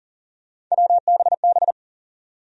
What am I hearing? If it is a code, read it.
Morse code, heard as a keyed tone, about 40 wpm, 700 Hz. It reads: WBB
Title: USA Call Sign Postfixes - 3 Letters 40wpm